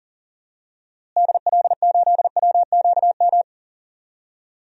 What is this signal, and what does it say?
Morse code, heard as a keyed tone, about 40 wpm, 700 Hz. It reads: DL8WQM
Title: DXCC-100 Call Signs - 6 Characters - Repeated 2x - 40wpm